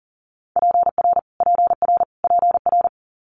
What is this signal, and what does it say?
Morse code, heard as a keyed tone, about 43 wpm, 700 Hz. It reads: PR PR PR
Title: Multi-letter Phonemes Repeated 3x 40wpm